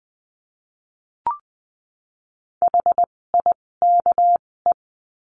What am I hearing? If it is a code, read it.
Morse code, heard as a keyed tone, about 20 wpm, spaced as for 16 wpm, 700 Hz. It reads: HIKE
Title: 256 Lesson 42a - Words 12-20wpm